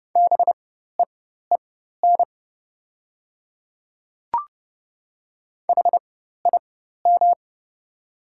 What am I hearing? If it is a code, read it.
Morse code, heard as a keyed tone, about 30 wpm, spaced as for 14 wpm, 700 Hz. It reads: BEEN HIM
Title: 4x Character Spacing - Top 200 Words - 30wpm